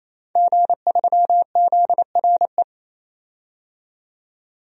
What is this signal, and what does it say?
Morse code, heard as a keyed tone, about 28 wpm, 700 Hz. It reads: G3ZRE
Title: DXCC-100 Call Signs - 5 Characters - 28wpm